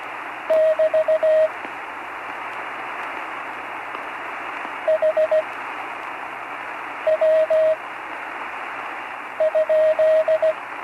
Morse code, heard as a keyed tone, about 16 wpm, spaced as for 5 wpm, 630 Hz, with radio noise. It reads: <BT> HW?
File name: CW Learning Programs / Learning Morse Code (from Tune in the World) / 43 on the air